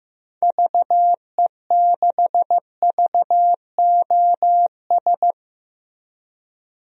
Morse code, heard as a keyed tone, about 15 wpm, 700 Hz. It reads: VE6VOS
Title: DXCC-100 Call Signs - 6 Characters - Repeated 2x - 15wpm